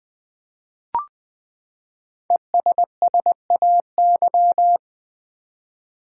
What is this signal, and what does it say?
Morse code, heard as a keyed tone, about 20 wpm, 700 Hz. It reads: ESSAY
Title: Five Letter Words - 20wpm